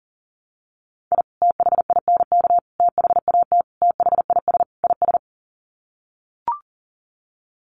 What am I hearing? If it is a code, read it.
Morse code, heard as a keyed tone, about 40 wpm, 700 Hz. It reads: I THINK THAT THIS IS
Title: Sets of 5 Words Encore2 40wpm